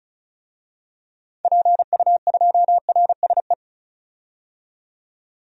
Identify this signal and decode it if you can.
Morse code, heard as a keyed tone, about 36 wpm, 700 Hz. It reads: PU2RSE
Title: DXCC-100 Call Signs - 6 Characters - 35wpm